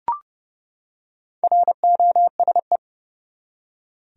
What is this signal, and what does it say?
Morse code, heard as a keyed tone, about 30 wpm, 700 Hz. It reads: ROSE
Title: USA Common Names - Max 4 Letters - 30wpm